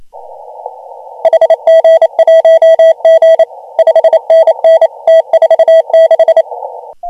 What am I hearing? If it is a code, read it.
Morse code, heard as a keyed tone, about 28 wpm, 3250 Hz, with noise. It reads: HG1G 5NN T46
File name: AUDIO2019 / HG1G CW